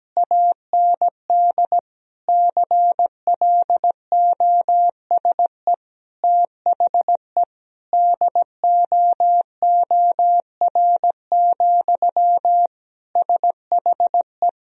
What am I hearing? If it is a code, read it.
Morse code, heard as a keyed tone, about 17 wpm, 700 Hz. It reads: AND CLOSE THE DOOR, SHE
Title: Sentences from Top 1000 Words - Review Part 1 - 17wpm